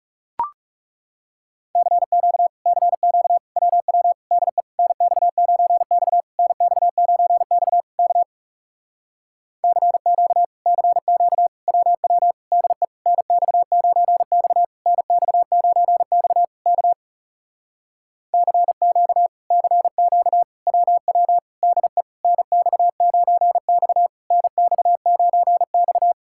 Morse code, heard as a keyed tone, about 40 wpm, 700 Hz. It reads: CQ CQ WW DE NX9X NX9X K CQ CQ WW DE NX9X NX9X K CQ CQ WW DE NX9X NX9X